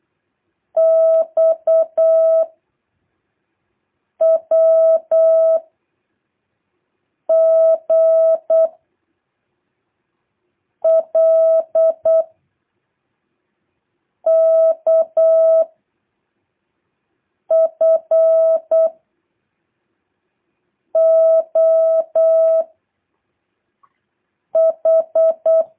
Morse code, heard as a keyed tone, about 8 wpm, 650 Hz. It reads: X W G L K F O H
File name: jh8cbh / 2023cw